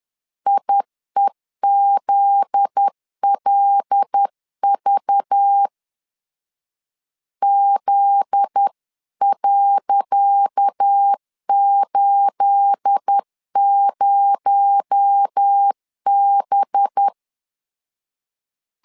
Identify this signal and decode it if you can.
Morse code, heard as a keyed tone, about 10 wpm, 800 Hz. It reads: IEZLV Z.80B
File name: LEZIONI / 10 SIGNS / 10C DOT 10wpm 2940